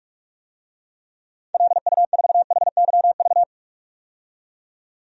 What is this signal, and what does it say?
Morse code, heard as a keyed tone, about 46 wpm, 700 Hz. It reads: LU4HYV
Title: DXCC-100 Call Signs - 6 Characters - 45wpm